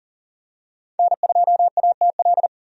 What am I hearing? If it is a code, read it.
Morse code, heard as a keyed tone, about 39 wpm, 700 Hz. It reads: N2ATL
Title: USA Call Signs - 5 Characters - Repeated 3x - 40wpm